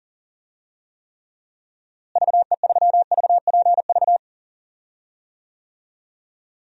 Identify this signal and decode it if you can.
Morse code, heard as a keyed tone, about 37 wpm, 700 Hz. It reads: VE3VPV
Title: DXCC-100 Call Signs - 6 Characters - Repeated 3x - 40wpm